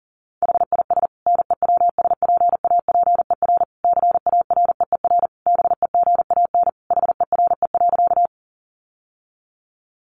Morse code, heard as a keyed tone, about 40 wpm, 700 Hz. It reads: HIS NEWSPAPER CAREER BEGAN HERE.
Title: Sentences from Top 1100 Words 40wpm